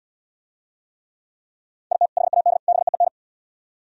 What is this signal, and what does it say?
Morse code, heard as a keyed tone, about 70 wpm, 700 Hz. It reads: IT HAD BEEN